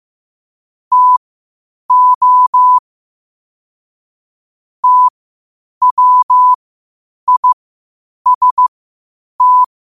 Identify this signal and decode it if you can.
Morse code, heard as a keyed tone, about 15 wpm, spaced as for 8 wpm, 1000 Hz. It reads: TO TWIST
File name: qotdfiles / 8wpm